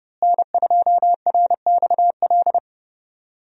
Morse code, heard as a keyed tone, about 32 wpm, 700 Hz. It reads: N2RXL
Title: USA Call Signs - 5 Characters - Repeated 2x - 30wpm